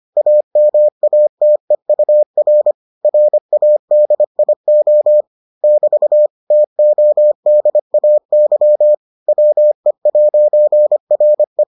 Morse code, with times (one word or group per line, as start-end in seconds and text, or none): 0.2-2.7 AMATEUR
3.0-5.2 RADIO
5.6-6.3 <BT>
6.5-8.9 TODAY
9.3-11.6 WE'RE